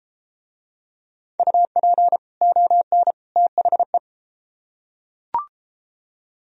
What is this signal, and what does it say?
Morse code, heard as a keyed tone, about 33 wpm, 700 Hz. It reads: UP ON THE